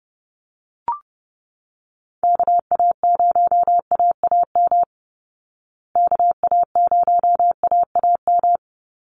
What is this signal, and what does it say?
Morse code, heard as a keyed tone, about 30 wpm, 700 Hz. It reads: KA0AAM KA0AAM
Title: Call Signs Repeated 3x - 30wpm